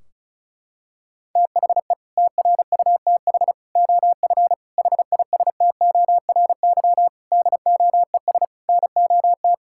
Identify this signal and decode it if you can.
Morse code, heard as a keyed tone, about 36 wpm, 700 Hz. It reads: THE TRUTH OF HISTORY DOES NOT
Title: Sentences from Top 1600 Words 35wpm